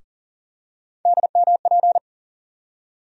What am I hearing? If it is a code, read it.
Morse code, heard as a keyed tone, about 40 wpm, 700 Hz. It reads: DMP
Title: USA Call Sign Postfixes - 3 Letters 40wpm